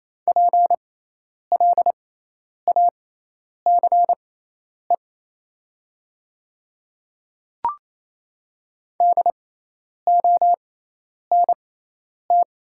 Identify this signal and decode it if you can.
Morse code, heard as a keyed tone, about 28 wpm, spaced as for 10 wpm, 700 Hz. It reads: PLACE DONT
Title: 6x Character Spacing - Top 300 Words - 28wpm